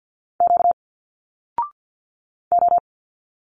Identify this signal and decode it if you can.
Morse code, heard as a keyed tone, about 50 wpm, 700 Hz. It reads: Q O
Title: Single Letters 50wpm